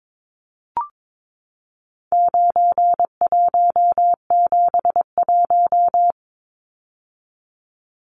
Morse code, heard as a keyed tone, about 22 wpm, 700 Hz. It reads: 9171
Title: Numbers - 4-Digit 22wpm